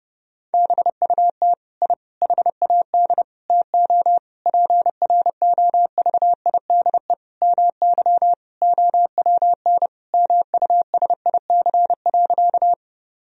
Morse code, heard as a keyed tone, about 32 wpm, 700 Hz. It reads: BUT I HAD TO PROVIDE MY OWN MUSIC.
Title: Sentences from Top 1000 Words - Review Part 3 - 30wpm